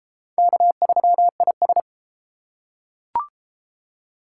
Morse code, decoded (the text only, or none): K3IS